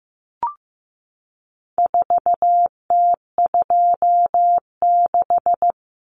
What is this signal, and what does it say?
Morse code, heard as a keyed tone, about 15 wpm, 700 Hz. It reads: 4T26
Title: Common Cut Numbers - 4-Digit 15wpm